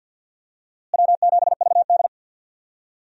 Morse code, heard as a keyed tone, about 46 wpm, 700 Hz. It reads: W7VD
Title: USA Call Signs - 4 Characters - Repeated 2x - 50wpm